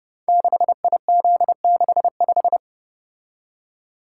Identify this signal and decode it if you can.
Morse code, heard as a keyed tone, about 28 wpm, 700 Hz. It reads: 6IZ65
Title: Code Groups - 5 Letter:Numbers - 30wpm